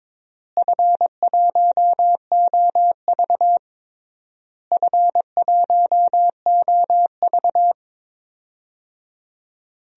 Morse code, heard as a keyed tone, about 22 wpm, 700 Hz. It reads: F1OV F1OV
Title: DXCC-100 Call Signs - 4 Characters - Repeated 3x - 22wpm